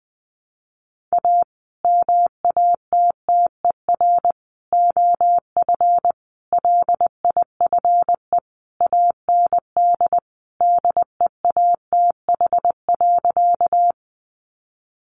Morse code, heard as a keyed tone, about 20 wpm, 700 Hz. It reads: A MATTER OF LIFE AND DEATH.
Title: Sentences from Top 1100 Words 20wpm